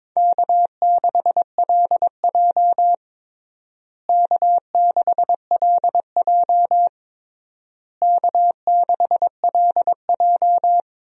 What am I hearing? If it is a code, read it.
Morse code, heard as a keyed tone, about 22 wpm, 700 Hz. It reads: K6LJ K6LJ K6LJ
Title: USA Call Signs - 4 Characters - Repeated 3x - 22wpm